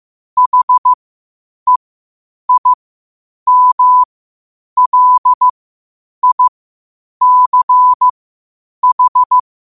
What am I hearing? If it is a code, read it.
Morse code, heard as a keyed tone, about 15 wpm, spaced as for 8 wpm, 1000 Hz. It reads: HEIMLICH
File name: qotdfiles / 8wpm